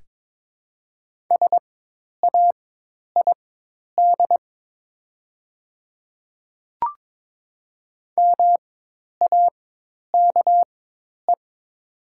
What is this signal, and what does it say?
Morse code, heard as a keyed tone, about 22 wpm, spaced as for 10 wpm, 700 Hz. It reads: SAID MAKE
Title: 4x Character Spacing - Top 100 Words - 22wpm